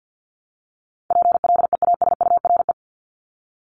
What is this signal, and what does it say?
Morse code, heard as a keyed tone, about 48 wpm, 700 Hz. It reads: PLEASURE
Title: Top 1501-1750 Words 50wpm